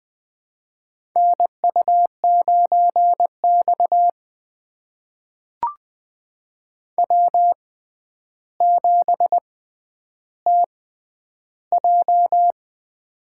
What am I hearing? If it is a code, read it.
Morse code, heard as a keyed tone, about 20 wpm, spaced as for 10 wpm, 700 Hz. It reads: NU9X W7TJ
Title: USA Call Signs - 4 Characters - 6x Character Spacing - 20wpm